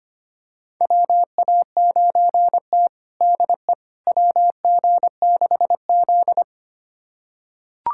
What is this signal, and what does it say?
Morse code, heard as a keyed tone, about 25 wpm, 700 Hz. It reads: WA9T DE WG6Z
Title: USA Callsign DE Callsign - 4 Characters - 25wpm